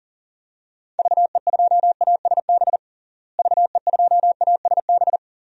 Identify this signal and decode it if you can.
Morse code, heard as a keyed tone, about 40 wpm, 700 Hz. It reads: VE2ASB VE2ASB
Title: DXCC-100 Call Signs - 6 Characters - Repeated 3x - 40wpm